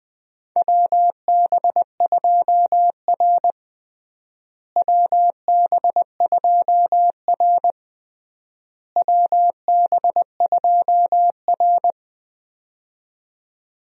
Morse code, heard as a keyed tone, about 20 wpm, 700 Hz. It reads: WB2R WB2R WB2R
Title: USA Call Signs - 4 Characters - Repeated 3x - 20wpm